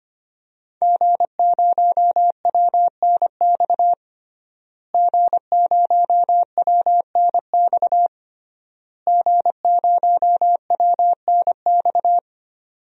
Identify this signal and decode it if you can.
Morse code, heard as a keyed tone, about 25 wpm, 700 Hz. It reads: G0WNX G0WNX G0WNX
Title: DXCC-100 Call Signs - 5 Characters - Repeated 3x - 25wpm